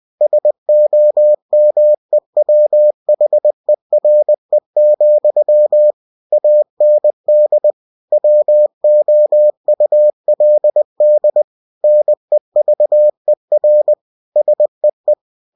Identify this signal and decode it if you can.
Morse code, heard as a keyed tone, about 20 wpm, 600 Hz. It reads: SOMEWHERE, AND WOULD NEVER SEE